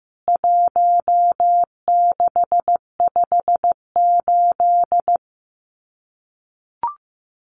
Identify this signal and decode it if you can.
Morse code, heard as a keyed tone, about 15 wpm, 700 Hz. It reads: 1658
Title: CWT Exchange 15wpm